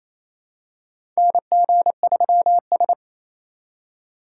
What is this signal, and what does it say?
Morse code, heard as a keyed tone, about 28 wpm, 700 Hz. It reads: NG3S
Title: USA Call Signs - 4 Characters - 3x Character Spacing - 28wpm